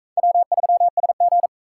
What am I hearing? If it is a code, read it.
Morse code, heard as a keyed tone, about 37 wpm, 700 Hz. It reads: W3SG